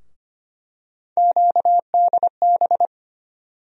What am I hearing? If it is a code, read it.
Morse code, heard as a keyed tone, about 25 wpm, 700 Hz. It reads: QDB
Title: USA Call Sign Postfixes - 3 Letters 25wpm